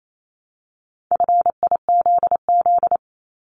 Morse code, heard as a keyed tone, about 28 wpm, 700 Hz. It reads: FIZZ
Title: Multi-letter Phonemes Repeated 3x 28wpm